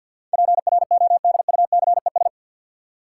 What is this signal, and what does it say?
Morse code, heard as a keyed tone, about 53 wpm, 700 Hz. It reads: PRODUCES
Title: Sentences from Top 1000 Words - Review Part 6 - New Word Drill - 50wpm